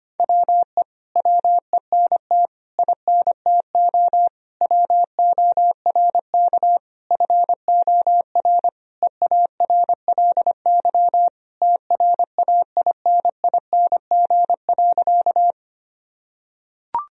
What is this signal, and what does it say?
Morse code, heard as a keyed tone, about 25 wpm, 700 Hz. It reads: WE WENT INTO WORK FOR EARLY TRAINING.
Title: Sentences from Top 1000 Words - Review Part 4 - 25wpm